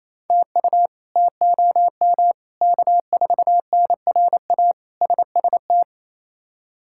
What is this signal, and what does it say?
Morse code, heard as a keyed tone, about 28 wpm, 700 Hz. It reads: TU TOM K4NRA SST
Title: K1USN SST QSO Practice - 28wpm